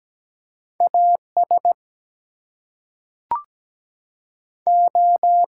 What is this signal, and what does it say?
Morse code, heard as a keyed tone, about 17 wpm, 700 Hz. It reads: AS O
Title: Two Letter Words - 17wpm